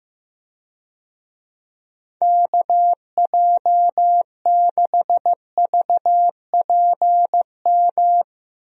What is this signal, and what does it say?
Morse code, heard as a keyed tone, about 16 wpm, 700 Hz. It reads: KJ6VPM
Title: USA Call Signs - 6 Characters - 15wpm